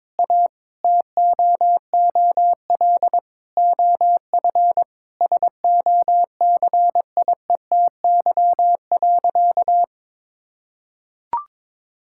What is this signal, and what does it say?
Morse code, heard as a keyed tone, about 22 wpm, 700 Hz. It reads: A TOOL OF SOCIETY.
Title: Sentences from Top 1000 Words - Review Part 6 - 22wpm